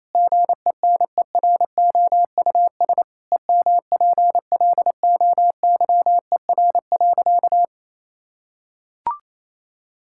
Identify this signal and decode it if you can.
Morse code, heard as a keyed tone, about 27 wpm, 700 Hz. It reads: GENEROUS EMPLOYER.
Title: Pangram Challenge v2 - 28wpm